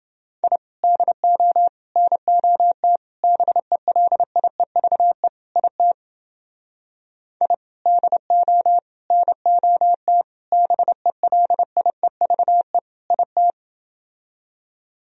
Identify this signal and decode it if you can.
Morse code, heard as a keyed tone, about 29 wpm, 700 Hz. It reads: I DO NOT BELIEVE IT I DO NOT BELIEVE IT